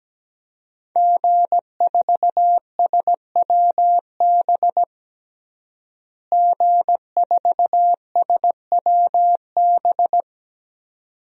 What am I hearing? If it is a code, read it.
Morse code, heard as a keyed tone, about 17 wpm, 700 Hz. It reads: G4SWB G4SWB
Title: DXCC-100 Call Signs - 5 Characters - Repeated 2x - 17wpm